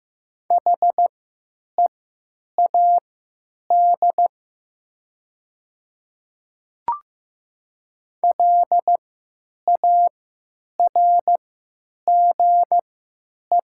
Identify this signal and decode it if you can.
Morse code, heard as a keyed tone, about 15 wpm, spaced as for 8 wpm, 700 Hz. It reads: HEAD LARGE
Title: 3x Character Spacing - Top 300 Words - 15wpm